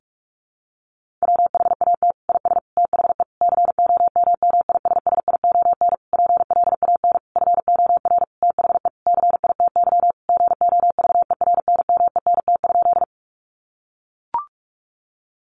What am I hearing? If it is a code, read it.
Morse code, heard as a keyed tone, about 44 wpm, 700 Hz. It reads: WHAT IS THE COMMISSION PLAN FOR THE CITY GOVERNMENT?
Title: Sentences from Top 1400 Words 45wpm